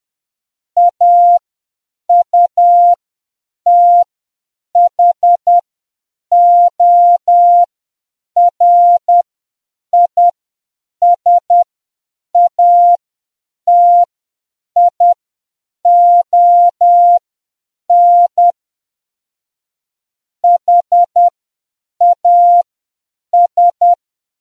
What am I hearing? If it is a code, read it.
Morse code, heard as a keyed tone, about 10 wpm, spaced as for 7 wpm, 700 Hz. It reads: AUTHORISATION HAS